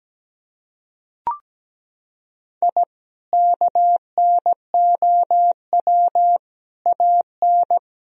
Morse code, heard as a keyed tone, about 17 wpm, 700 Hz. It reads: I KNOW AN
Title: Sentences from Top 1000 Words - Review Part 3 - 17wpm